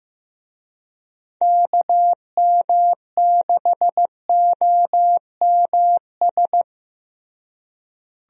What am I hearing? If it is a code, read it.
Morse code, heard as a keyed tone, about 15 wpm, 700 Hz. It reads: KM6OMS
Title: USA Call Signs - 6 Characters - Repeated 2x - 15wpm